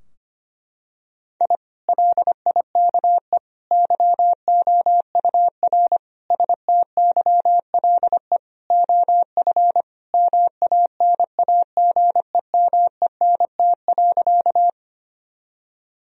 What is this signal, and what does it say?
Morse code, heard as a keyed tone, about 25 wpm, 700 Hz. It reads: I LIKE YOUR STYLE OF MANAGEMENT.
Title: Sentences from Top 1000 Words - Review Part 4 - 25wpm